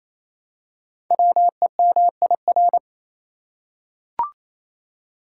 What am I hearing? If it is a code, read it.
Morse code, heard as a keyed tone, about 27 wpm, 700 Hz. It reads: WEMIR
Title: Code Groups - 5 Letters - 28wpm